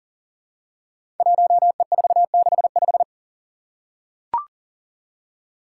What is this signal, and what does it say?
Morse code, heard as a keyed tone, about 39 wpm, 700 Hz. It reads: 1E465
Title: Code Groups - 5 Letter:Numbers - 40wpm